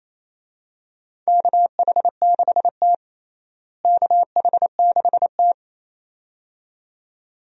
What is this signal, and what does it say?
Morse code, heard as a keyed tone, about 28 wpm, 700 Hz. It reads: KH6T KH6T
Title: DXCC-100 Call Signs - 4 Characters - Repeated 2x - 28wpm